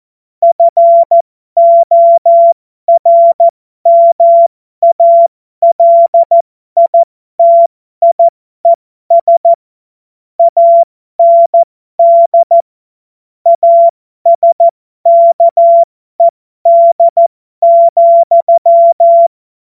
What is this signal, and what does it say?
Morse code, heard as a keyed tone, about 14 wpm, spaced as for 12 wpm, 680 Hz. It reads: FORMALITIES AND ASKED,